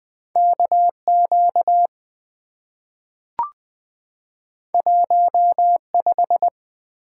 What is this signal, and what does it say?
Morse code, heard as a keyed tone, about 20 wpm, 700 Hz. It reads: KQ 15
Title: Two Letter-Number Combinations - 20wpm